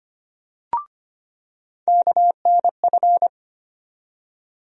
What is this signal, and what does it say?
Morse code, heard as a keyed tone, about 25 wpm, 700 Hz. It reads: KNF
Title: USA Call Sign Postfixes - 3 Letters 25wpm